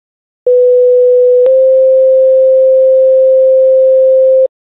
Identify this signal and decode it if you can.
Two-tone page, heard as a one-second tone, then a three-second tone, about 500 then 520 Hz. It is heard from 0.5 to 4.5 s.